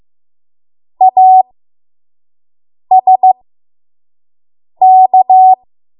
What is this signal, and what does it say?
Morse code, heard as a keyed tone, about 15 wpm, spaced as for 5 wpm, 750 Hz. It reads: ASK
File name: Morse / 05WPM